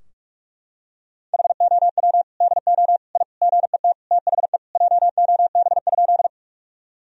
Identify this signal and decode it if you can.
Morse code, heard as a keyed tone, about 46 wpm, 700 Hz. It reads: HOW DO I GET THE JOB?
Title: Sentences from Top 1000 Words - Review Part 3 - 45wpm